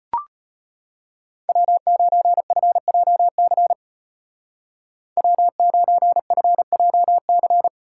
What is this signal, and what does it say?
Morse code, heard as a keyed tone, about 33 wpm, 700 Hz. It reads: W9FJC W9FJC